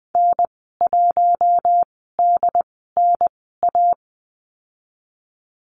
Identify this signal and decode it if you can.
Morse code, heard as a keyed tone, about 20 wpm, spaced as for 14 wpm, 700 Hz. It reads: N1DNA
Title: 069 Lesson 12g - Letter d - Call Signs 20wpm